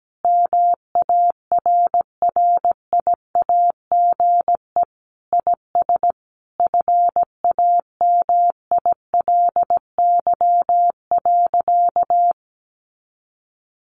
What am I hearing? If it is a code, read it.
Morse code, heard as a keyed tone, about 17 wpm, 700 Hz. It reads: MARRIAGE IS FAMILY.
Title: Sentences from Top 800 Words 17wpm